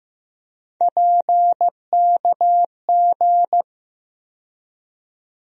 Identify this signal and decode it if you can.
Morse code, heard as a keyed tone, about 16 wpm, 700 Hz. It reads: PKG
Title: USA Call Sign Postfixes - 3 Letters 15wpm